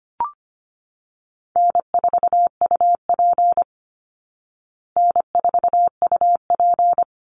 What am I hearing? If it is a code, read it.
Morse code, heard as a keyed tone, about 25 wpm, 700 Hz. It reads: N4UP N4UP
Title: Easy Call Signs Repeated 2x - 25wpm